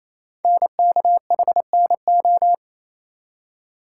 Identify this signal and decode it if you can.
Morse code, heard as a keyed tone, about 27 wpm, 700 Hz. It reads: NKHNO
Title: Code Groups - 5 Letters - 28wpm